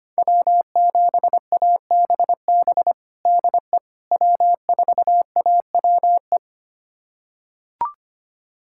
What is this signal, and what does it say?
Morse code, heard as a keyed tone, about 25 wpm, 700 Hz. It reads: W7ABB DE W4AWE
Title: USA Callsign DE Callsign - 5 Characters - 25wpm